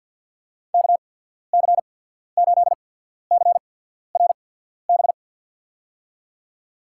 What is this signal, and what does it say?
Morse code, heard as a keyed tone, about 52 wpm, spaced as for 14 wpm, 690 Hz. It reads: KC8CRB
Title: USA Call Signs - 6 Characters - 8x Character Spacing - 50wpm